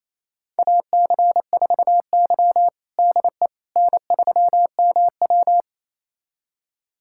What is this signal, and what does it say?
Morse code, heard as a keyed tone, about 28 wpm, 700 Hz. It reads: AC4Y DE N3MW
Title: USA Callsign DE Callsign - 4 Characters - 28wpm